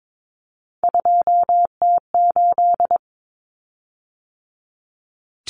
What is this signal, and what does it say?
Morse code, heard as a keyed tone, about 22 wpm, 700 Hz. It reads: 2T8
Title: Common Cut Numbers - 3-Digit 22wpm